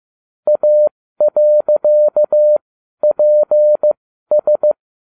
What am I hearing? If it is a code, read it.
Morse code, heard as a keyed tone, about 15 wpm, spaced as for 12 wpm, 600 Hz. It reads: A.PS